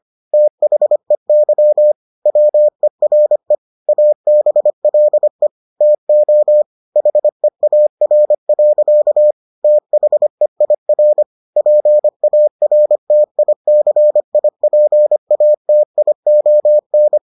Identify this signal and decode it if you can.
Morse code, heard as a keyed tone, about 25 wpm, 600 Hz. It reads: THEY WERE ABLE TO HEAR. THEIR PARTICIPATION